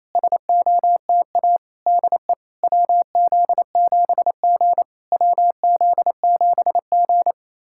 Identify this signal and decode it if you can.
Morse code, heard as a keyed tone, about 28 wpm, 690 Hz. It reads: SOTA DE WZ7G WZ7G
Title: SOTA QSO Practice 28wpm